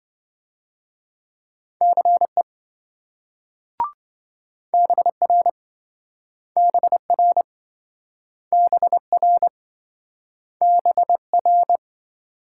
Morse code, heard as a keyed tone, about 27 wpm, 700 Hz. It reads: CE BR BR BR BR